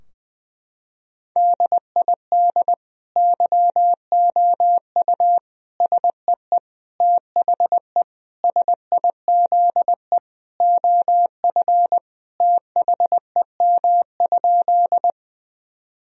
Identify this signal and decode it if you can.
Morse code, heard as a keyed tone, about 21 wpm, 700 Hz. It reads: DID YOU SEE THE SIZE OF THEM?
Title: Sentences from Top 1000 Words - Review Part 2 - 20wpm